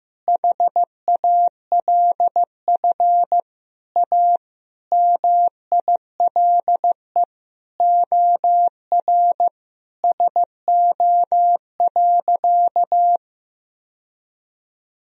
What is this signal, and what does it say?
Morse code, heard as a keyed tone, about 15 wpm, 700 Hz. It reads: HALF A MILE OR SO.
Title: Sentences from Top 1000 Words - Review Part 1 - 15wpm